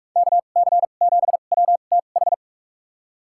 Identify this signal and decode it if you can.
Morse code, heard as a keyed tone, about 42 wpm, spaced as for 31 wpm, 700 Hz. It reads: KC7WTH
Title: USA Call Signs - 6 Characters - 2x Character Spacing - 45wpm